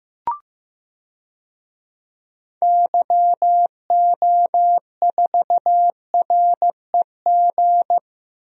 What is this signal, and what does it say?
Morse code, heard as a keyed tone, about 15 wpm, 700 Hz. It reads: YO4REG
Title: DXCC-100 Call Signs - 6 Characters - Repeated 3x - 15wpm